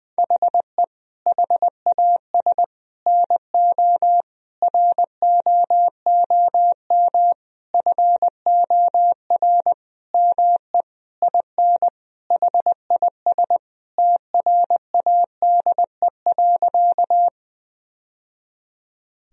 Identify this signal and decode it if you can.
Morse code, heard as a keyed tone, about 20 wpm, 700 Hz. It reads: HE HAS NO ROOM FOR ME IN HIS TRADE.
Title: Sentences from Top 1000 Words - Review Part 4 - 20wpm